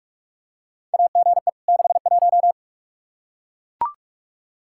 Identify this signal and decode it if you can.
Morse code, heard as a keyed tone, about 41 wpm, 700 Hz. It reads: AGE 61
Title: K1USN SST QSO Practice 45wpm